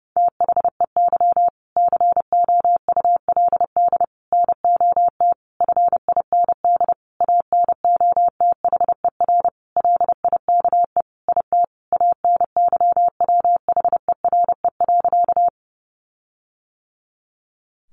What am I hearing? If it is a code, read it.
Morse code, heard as a keyed tone, about 30 wpm, 700 Hz. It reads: THEY COULD NOT FIND ANOTHER LIKE IT ANYWHERE.
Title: Sentences from Top 1400 Words 30wpm